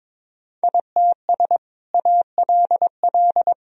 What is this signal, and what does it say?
Morse code, heard as a keyed tone, about 22 wpm, 700 Hz. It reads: ITS ALL
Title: Sentences from Top 1000 Words - Review Part 2 - 22wpm